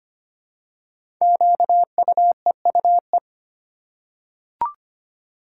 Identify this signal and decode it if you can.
Morse code, heard as a keyed tone, about 25 wpm, 700 Hz. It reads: QUEUE
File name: 5 Letter Words - First 3 Letters Given - 25wpm